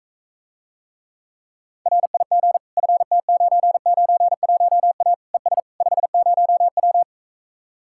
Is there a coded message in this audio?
Yes: Morse code from 1.9 to 7.0 s, heard as a keyed tone, about 45 wpm, 700 Hz. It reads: RIG FT991A ES 50W